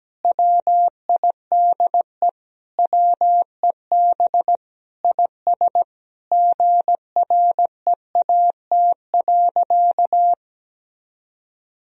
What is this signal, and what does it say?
Morse code, heard as a keyed tone, about 17 wpm, 700 Hz. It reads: WIDE WEB IS GREAT.
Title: Sentences from Top 1000 Words - Review Part 4 - 17wpm